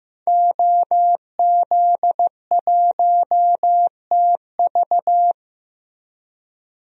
Morse code, heard as a keyed tone, about 15 wpm, 700 Hz. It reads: OZ1TV
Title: DXCC-100 Call Signs - 5 Characters - Repeated 3x - 15wpm